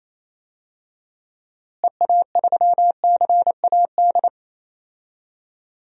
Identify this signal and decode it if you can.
Morse code, heard as a keyed tone, about 28 wpm, 700 Hz. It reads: EA3CAD
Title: DXCC-100 Call Signs - 6 Characters - Repeated 3x - 28wpm